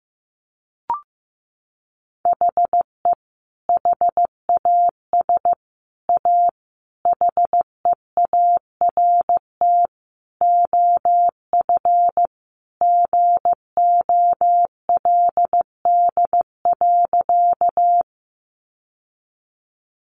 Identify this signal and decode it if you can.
Morse code, heard as a keyed tone, about 15 wpm, 700 Hz. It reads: HE HAS A HEART OF GOLD.
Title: Sentences from Top 1200 Words 15wpm